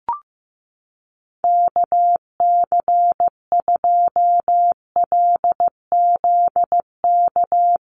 Morse code, heard as a keyed tone, about 15 wpm, 700 Hz. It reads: KC2LZK
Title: USA Call Signs 15wpm